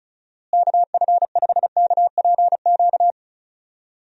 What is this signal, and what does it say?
Morse code, heard as a keyed tone, about 34 wpm, 700 Hz. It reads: KF5KPQ
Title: USA Call Signs - 6 Characters - Repeated 2x - 35wpm